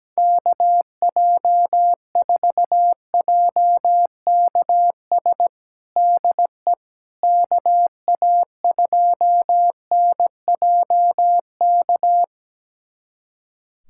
Morse code, heard as a keyed tone, about 17 wpm, 700 Hz. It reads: KJ4JKS DE KA2NJK
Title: USA Callsign DE Callsign - 6 Characters - 17wpm